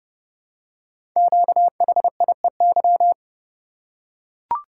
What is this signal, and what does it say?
Morse code, heard as a keyed tone, about 29 wpm, 700 Hz. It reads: QHIEY
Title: Code Groups - 5 Letters - 30wpm